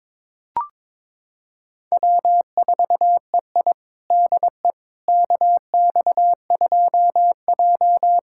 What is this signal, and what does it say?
Morse code, heard as a keyed tone, about 21 wpm, 700 Hz. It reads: W4EI DE KX2J
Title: USA Callsign DE Callsign - 4 Characters - 22wpm